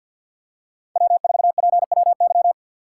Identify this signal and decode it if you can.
Morse code, heard as a keyed tone, about 52 wpm, 700 Hz. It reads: W4PWY
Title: USA Call Signs - 5 Characters - 3x Character Spacing - 50wpm